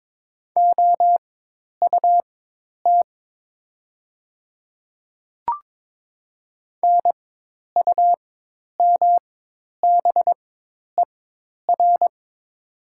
Morse code, heard as a keyed tone, about 22 wpm, spaced as for 10 wpm, 700 Hz. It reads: OUT NUMBER
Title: 4x Character Spacing - Top 500 Words - 22wpm